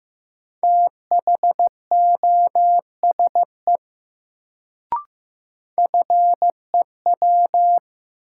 Morse code, heard as a keyed tone, about 15 wpm, 700 Hz. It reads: THOSE FEW
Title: Sentences from Top 300 Words - New Word Drill - 15wpm